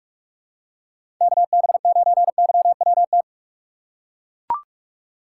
Morse code, heard as a keyed tone, about 42 wpm, 700 Hz. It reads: KB9YWT
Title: USA Call Signs - 6 Characters - 3x Character Spacing - 45wpm